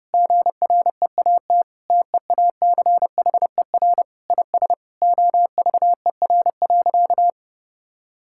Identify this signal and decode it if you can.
Morse code, heard as a keyed tone, about 32 wpm, 700 Hz. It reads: GREAT TEACHER IS OVER.
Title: Sentences from Top 1600 Words 30wpm